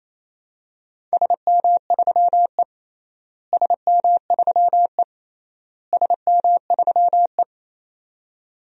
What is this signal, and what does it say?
Morse code, heard as a keyed tone, about 28 wpm, 700 Hz. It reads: SM3E SM3E SM3E
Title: DXCC-100 Call Signs - 4 Characters - Repeated 3x - 28wpm